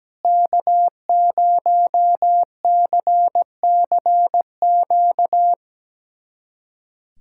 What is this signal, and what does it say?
Morse code, heard as a keyed tone, about 17 wpm, 700 Hz. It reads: K0CCQ
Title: USA Callsign DE Callsign - 5 Characters - 17wpm